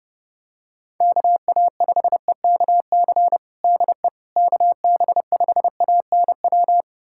Morse code, heard as a keyed tone, about 30 wpm, 700 Hz. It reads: KA5EKC DE KB5ANW
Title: USA Callsign DE Callsign - 6 Characters - 30wpm